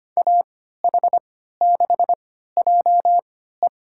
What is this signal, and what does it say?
Morse code, heard as a keyed tone, about 25 wpm, spaced as for 14 wpm, 700 Hz. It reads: AH6JE
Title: USA Call Signs - 5 Characters - 3x Character Spacing - 25wpm